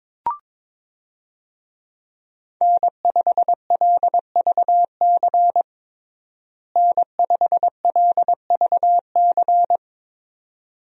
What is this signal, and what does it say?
Morse code, heard as a keyed tone, about 22 wpm, 700 Hz. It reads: N5LVC N5LVC
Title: USA Call Signs - 5 Characters - Repeated 2x - 22wpm